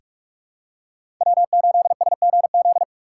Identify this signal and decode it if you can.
Morse code, heard as a keyed tone, about 41 wpm, 700 Hz. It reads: W8SGZ
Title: USA Call Signs - 5 Characters - 4x Character Spacing - 45wpm